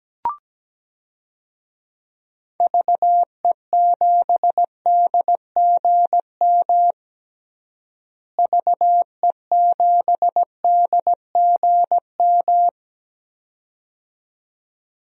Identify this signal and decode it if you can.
Morse code, heard as a keyed tone, about 17 wpm, 700 Hz. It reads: VE7DGM VE7DGM
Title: DXCC-100 Call Signs - 6 Characters - Repeated 2x - 17wpm